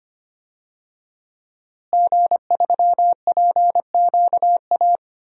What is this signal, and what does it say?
Morse code, heard as a keyed tone, about 25 wpm, 700 Hz. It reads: G3PQA
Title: DXCC-100 Call Signs - 5 Characters - Repeated 2x - 25wpm